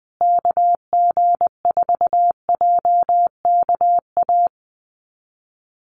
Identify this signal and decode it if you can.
Morse code, heard as a keyed tone, about 20 wpm, 700 Hz. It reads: KG4JKA
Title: USA Call Signs Repeated 2x 20wpm